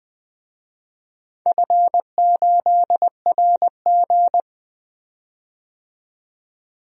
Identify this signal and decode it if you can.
Morse code, heard as a keyed tone, about 20 wpm, 700 Hz. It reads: F8RG
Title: DXCC-100 Call Signs - 4 Characters - 20wpm